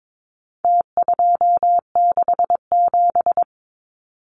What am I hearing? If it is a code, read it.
Morse code, heard as a keyed tone, about 22 wpm, 700 Hz. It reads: T267
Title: Numbers - Contest Style - 4-Digit 22wpm